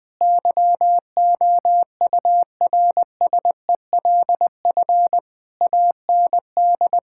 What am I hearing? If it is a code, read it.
Morse code, heard as a keyed tone, about 20 wpm, 700 Hz. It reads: YOURSELF AND
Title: Sentences from Top 1000 Words - Review Part 6 - 20wpm